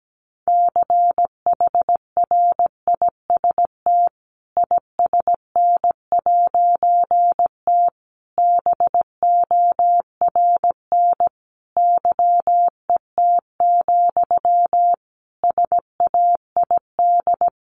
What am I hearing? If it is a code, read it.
Morse code, heard as a keyed tone, about 17 wpm, 700 Hz. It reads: CHRIST ISN'T BORN YET, SAID
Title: A Sound Of Thunder 17wpm